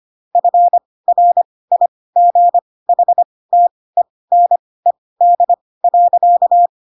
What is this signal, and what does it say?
Morse code, heard as a keyed tone, about 25 wpm, spaced as for 18 wpm, 700 Hz. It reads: FRIGHTENED.
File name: Pinocchio / Pinocchio ch4 12 25 ik1hge